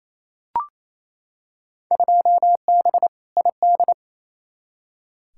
Morse code, heard as a keyed tone, about 28 wpm, 700 Hz. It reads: 2B ID
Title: Field Day Exchanges Encore - 28wpm